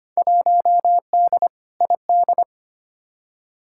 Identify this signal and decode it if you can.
Morse code, heard as a keyed tone, about 25 wpm, 700 Hz. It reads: 1D ID
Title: Field Day Exchanges - 25wpm